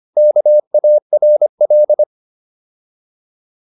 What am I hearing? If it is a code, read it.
Morse code, heard as a keyed tone, about 25 wpm, 600 Hz. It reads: KARL